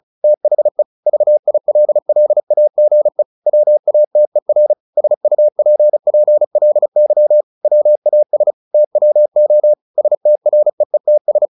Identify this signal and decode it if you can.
Morse code, heard as a keyed tone, about 35 wpm, 600 Hz. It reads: THE VILLAGE WATER SUPPLY WAS TWO STREETS
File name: the old man and the sea / the old man and the sea 1 35wpm